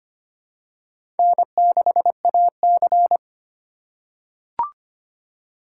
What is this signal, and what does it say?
Morse code, heard as a keyed tone, about 25 wpm, 700 Hz. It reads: N6AC
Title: USA Call Signs - 4 Characters - 4x Character Spacing - 25wpm